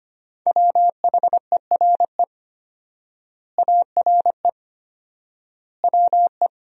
Morse code, heard as a keyed tone, about 25 wpm, 700 Hz. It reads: WHERE ARE WE
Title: 4x Word-Spacing - Sentences from Top 200 Words - 25wpm